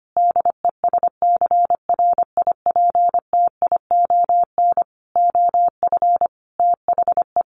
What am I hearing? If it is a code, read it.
Morse code, heard as a keyed tone, about 25 wpm, 700 Hz. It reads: DESCRIPTION OF THE
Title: Sentences from Top 1300 Words 25wpm